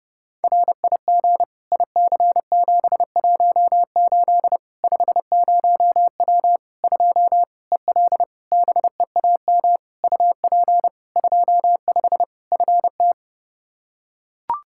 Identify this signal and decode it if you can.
Morse code, heard as a keyed tone, about 30 wpm, 700 Hz. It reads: RIG IC718 50W 2 EL BEAM UP 25 FT
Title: K1USN SST QSO Practice 30wpm